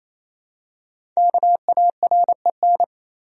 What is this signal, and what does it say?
Morse code, heard as a keyed tone, about 28 wpm, 700 Hz. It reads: KAREN
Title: USA Common Names - Max 5 Letters - 28wpm